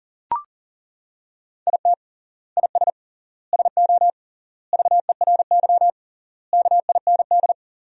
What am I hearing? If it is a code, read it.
Morse code, heard as a keyed tone, about 40 wpm, 700 Hz. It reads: IT IS SO VERY KIND
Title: 3x Word-Spacing - Sentences from Top 300 Words - 40wpm